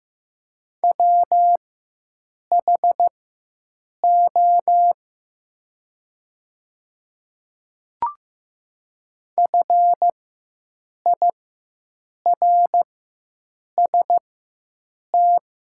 Morse code, heard as a keyed tone, about 16 wpm, spaced as for 7 wpm, 700 Hz. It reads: WHO FIRST
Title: 4x Character Spacing - Top 100 Words - 15wpm